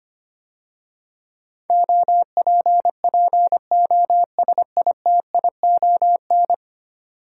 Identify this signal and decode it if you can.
Morse code, heard as a keyed tone, about 25 wpm, 700 Hz. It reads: OPPOSITION
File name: 10 Letter Words - First 3 Letters Given - 25wpm